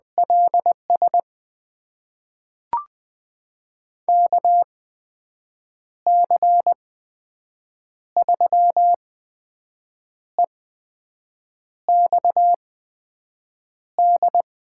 Morse code, heard as a keyed tone, about 20 wpm, spaced as for 6 wpm, 700 Hz. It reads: LS KC3EXD
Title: USA Call Signs - 6 Characters - 8x Character Spacing - 20wpm